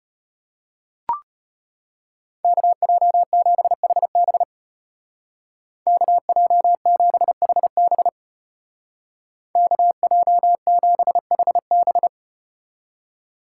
Morse code, heard as a keyed tone, about 34 wpm, 700 Hz. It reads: KJ7HB KJ7HB KJ7HB